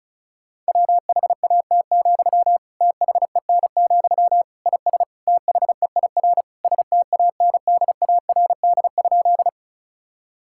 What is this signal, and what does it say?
Morse code, heard as a keyed tone, about 36 wpm, 700 Hz. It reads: WHAT, THEN, IS THEIR STANDARD?
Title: Sentences from Top 1000 Words - Review Part 6 - 35wpm